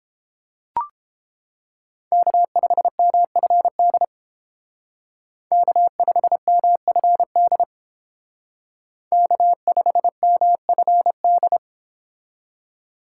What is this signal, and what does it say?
Morse code, heard as a keyed tone, about 29 wpm, 700 Hz. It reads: K5MFD K5MFD K5MFD